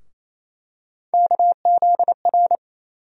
Morse code, heard as a keyed tone, about 28 wpm, 700 Hz. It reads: KZR
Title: USA Call Sign Postfixes - 3 Letters 28wpm